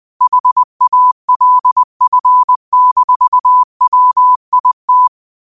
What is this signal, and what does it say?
Morse code, heard as a keyed tone, about 20 wpm, 1000 Hz. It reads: HALF-WIT
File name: qotdfiles / 20wpm